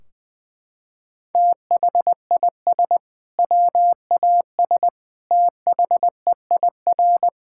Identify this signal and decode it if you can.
Morse code, heard as a keyed tone, about 20 wpm, 700 Hz. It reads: THIS WAS THEIR
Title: Sentences from Top 1600 Words 20wpm